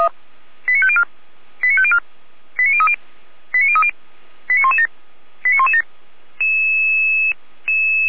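Five-tone selective call (ZVEI), heard as quick runs of five tones, tones of about 70 ms. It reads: 89504 89504 89030 89030 89117 89117
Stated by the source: ZVEI five-tone call 89115 received on an SDR.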